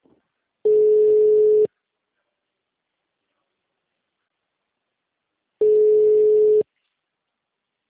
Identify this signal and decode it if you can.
Ringback tone (Europe, 425 Hz).